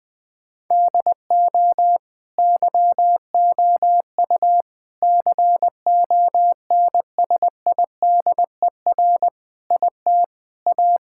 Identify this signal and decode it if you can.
Morse code, heard as a keyed tone, about 20 wpm, 700 Hz. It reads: DO YOU CONSIDER IT A
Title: Sentences from Top 1600 Words 20wpm